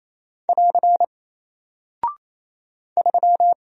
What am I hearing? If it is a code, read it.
Morse code, heard as a keyed tone, about 28 wpm, 700 Hz. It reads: <AR> 3
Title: Single Character (Letter:Number:Punctuation:Prosign) Alt 28wpm